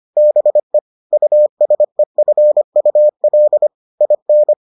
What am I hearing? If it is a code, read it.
Morse code, heard as a keyed tone, about 25 wpm, 600 Hz. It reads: BE USEFUL IN